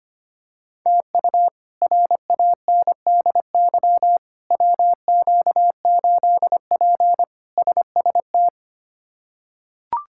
Morse code, heard as a keyed tone, about 25 wpm, 700 Hz. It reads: TU RANDY WQ8P SST
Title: K1USN SST QSO Practice - 25wpm